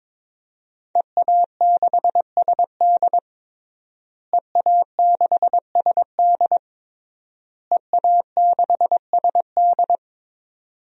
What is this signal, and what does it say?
Morse code, heard as a keyed tone, about 22 wpm, 700 Hz. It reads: EA6SD EA6SD EA6SD
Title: DXCC-100 Call Signs - 5 Characters - Repeated 3x - 22wpm